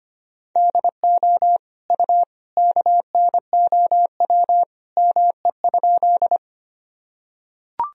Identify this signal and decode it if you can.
Morse code, heard as a keyed tone, about 25 wpm, 700 Hz. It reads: DO U KNOW ME?
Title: Sentences from Top 200 Words and Abbreviations 25wpm